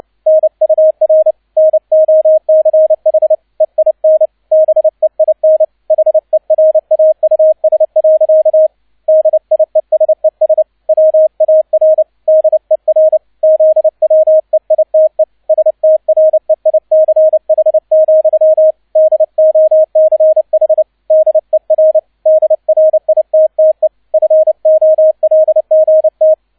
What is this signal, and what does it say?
Morse code, heard as a keyed tone, about 29 wpm, 630 Hz, with radio noise. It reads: NUR NOCH EIN BEIN HERAUS. DIESES WAR DER ZWEITE STREICH, DOCH DER DRITTE FOLGT
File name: 002 Zweiter Streich 30wpm